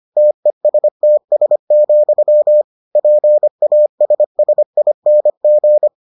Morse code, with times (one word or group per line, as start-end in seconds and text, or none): 0.2-2.6 TESTS,
3.0-5.9 PASSING